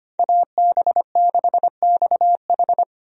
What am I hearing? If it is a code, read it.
Morse code, heard as a keyed tone, about 25 wpm, 700 Hz. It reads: AB6XH
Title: USA Call Signs - 5 Characters - 25wpm